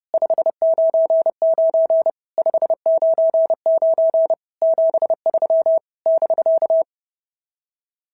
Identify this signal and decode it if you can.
Morse code, heard as a keyed tone, about 30 wpm, 650 Hz. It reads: 599 599 73 <BK>
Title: SOTA QSO Practice 30wpm